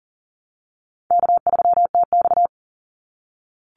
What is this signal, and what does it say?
Morse code, heard as a keyed tone, about 40 wpm, 700 Hz. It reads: K3TX
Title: Call Signs - 40wpm